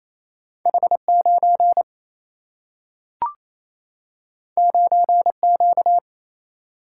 Morse code, heard as a keyed tone, about 28 wpm, 700 Hz. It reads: H 9 9 Q
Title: Single-Letter Single-Number 28wpm